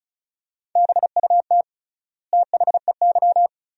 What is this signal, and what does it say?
Morse code, heard as a keyed tone, about 36 wpm, 700 Hz. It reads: BUT THEY
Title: 3x Word-Spacing - Sentences from Top 500 Words - 35wpm